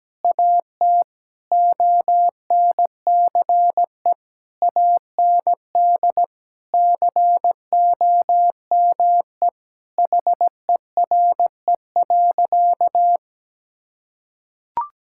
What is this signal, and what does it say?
Morse code, heard as a keyed tone, about 17 wpm, 700 Hz. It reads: AT ONCE AND COME HERE.
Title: Sentences from Top 1600 Words 17wpm